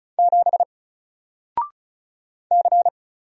Morse code, heard as a keyed tone, about 36 wpm, 700 Hz. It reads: 7 C
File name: Single-Letter-Number-35wpm